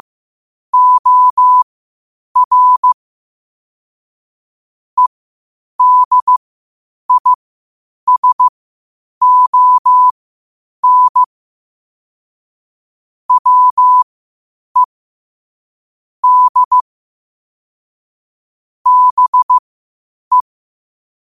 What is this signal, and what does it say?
Morse code, heard as a keyed tone, about 15 wpm, spaced as for 8 wpm, 1000 Hz. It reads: OR EDISON WE D BE
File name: qotdfiles / 8wpm